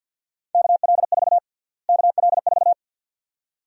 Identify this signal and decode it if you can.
Morse code, heard as a keyed tone, about 48 wpm, 700 Hz. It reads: KL4 KL4
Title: USA Call Sign Prefixes - Repeated 2x - 50wpm